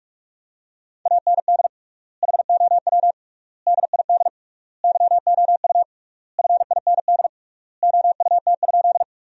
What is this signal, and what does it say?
Morse code, heard as a keyed tone, about 46 wpm, 700 Hz. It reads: AND HOW DID YOU FIND OUT?
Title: 3x Word-Spacing - Sentences from Top 100 Words - 45wpm